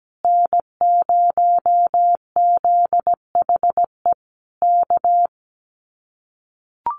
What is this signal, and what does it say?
Morse code, heard as a keyed tone, about 17 wpm, 700 Hz. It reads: N0ZHE K
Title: Common QSO Phrases Encore 17wpm